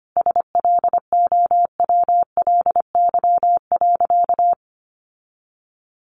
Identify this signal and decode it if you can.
Morse code, heard as a keyed tone, about 25 wpm, 700 Hz. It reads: SLOWLY.
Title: Sentences from Top 1100 Words 25wpm